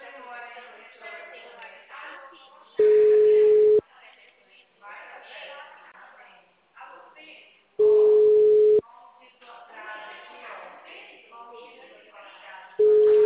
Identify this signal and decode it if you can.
Ringback tone (Europe, 425 Hz), with speech.